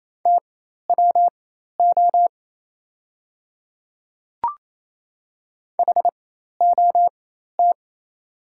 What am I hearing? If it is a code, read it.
Morse code, heard as a keyed tone, about 28 wpm, spaced as for 13 wpm, 700 Hz. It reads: TWO HOT
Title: 4x Character Spacing - Top 100 Words - 28wpm